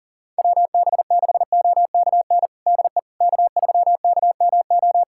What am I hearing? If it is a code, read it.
Morse code, heard as a keyed tone, about 40 wpm, 700 Hz. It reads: WB6OKN DE K3KMO
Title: USA Callsign DE Callsign 40wpm